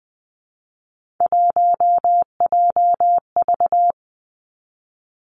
Morse code, heard as a keyed tone, about 20 wpm, 700 Hz. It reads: 1JV
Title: Three Challenging Letter-Number Combinations 20wpm